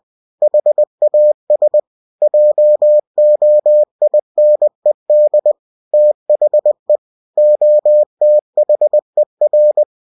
Morse code, heard as a keyed tone, about 20 wpm, 600 Hz. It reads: HAS JOINED THE OTHER